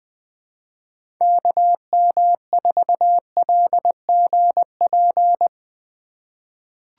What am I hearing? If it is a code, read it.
Morse code, heard as a keyed tone, about 20 wpm, 700 Hz. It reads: KM4LGP
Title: USA Call Signs - 6 Characters - 4x Character Spacing - 20wpm